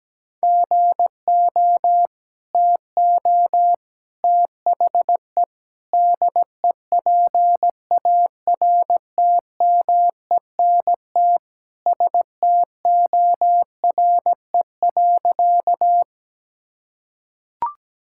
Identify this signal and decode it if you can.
Morse code, heard as a keyed tone, about 17 wpm, 700 Hz. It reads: GO TO THE DEPARTMENT STORE.
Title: Sentences from Top 1000 Words - Review Part 5 - 17wpm